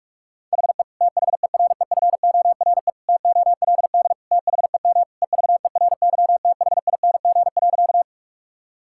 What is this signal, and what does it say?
Morse code, heard as a keyed tone, about 46 wpm, 690 Hz. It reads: HE THEREFORE TOLD THEM EVERYTHING.
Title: Sentences from Top 1000 Words - Review Part 5 - 45wpm